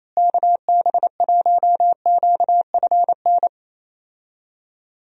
Morse code, heard as a keyed tone, about 28 wpm, 700 Hz. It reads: KB1QFN
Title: USA Call Signs - 6 Characters - Repeated 2x - 28wpm